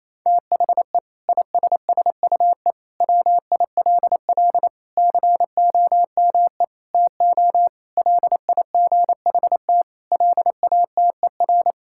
Morse code, heard as a keyed tone, about 27 wpm, 700 Hz. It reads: THE ISSUE WILL COME TO LIGHT LATER
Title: Sentences from Top 1000 Words - Review Part 6 - 28wpm